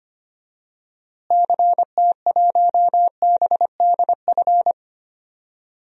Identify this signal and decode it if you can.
Morse code, heard as a keyed tone, about 25 wpm, 700 Hz. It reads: CT1BDF
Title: DXCC-100 Call Signs - 6 Characters - Repeated 3x - 25wpm